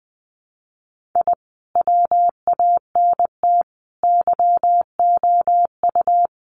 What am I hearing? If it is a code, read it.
Morse code, heard as a keyed tone, about 20 wpm, 700 Hz. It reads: I WANT YOU
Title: Sets of 4 Words 20wpm